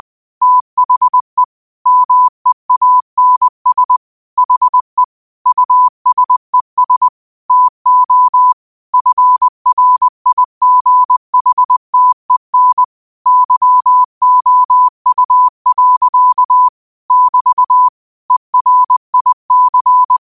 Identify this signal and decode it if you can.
Morse code, heard as a keyed tone, about 20 wpm, 1000 Hz. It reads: THE MEANS HE USES TO FRIGHTEN YOU. <BT> ERIC
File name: qotdfiles / 20wpm